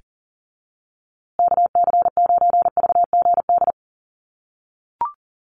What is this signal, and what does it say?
Morse code, heard as a keyed tone, about 40 wpm, 700 Hz. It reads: KC9VGD
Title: USA Call Signs Repeated 3x 40wpm